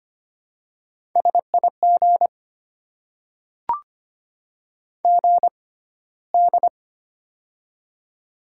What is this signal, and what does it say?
Morse code, heard as a keyed tone, about 25 wpm, spaced as for 15 wpm, 700 Hz. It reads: SIG GD
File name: Top 90 QSO Elements - 6x Character Spacing - 25wpm